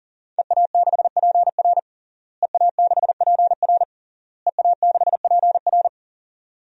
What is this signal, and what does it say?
Morse code, heard as a keyed tone, about 40 wpm, 700 Hz. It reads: EA6PR EA6PR EA6PR
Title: DXCC-100 Call Signs - 5 Characters - Repeated 3x - 40wpm